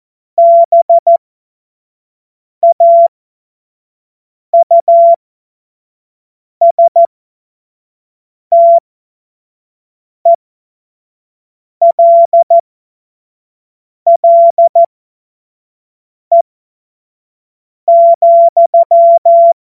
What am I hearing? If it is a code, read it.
Morse code, heard as a keyed tone, about 14 wpm, spaced as for 5 wpm, 680 Hz. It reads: BAUSTELLE,
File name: Morsekurs / Witze / Witz 31.Oktober 2023 25 F70